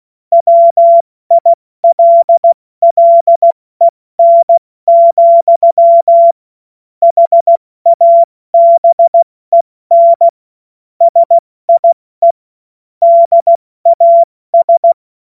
Morse code, heard as a keyed tone, about 16 wpm, 680 Hz. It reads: WILLEN, HABEN SIE DAS
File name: Morsekurs / Witze / Witz 11. Juli 2023 70 F80